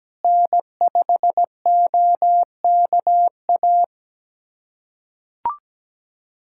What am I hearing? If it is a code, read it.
Morse code, heard as a keyed tone, about 17 wpm, 700 Hz. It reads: N5OKA
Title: USA Callsign DE Callsign - 5 Characters - 17wpm